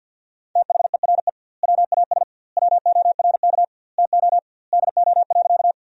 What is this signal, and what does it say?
Morse code, heard as a keyed tone, about 49 wpm, 700 Hz. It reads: THERE WAS WORK TO DO.
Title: 2x Word-Spacing - Sentences from Top 200 Words - 50wpm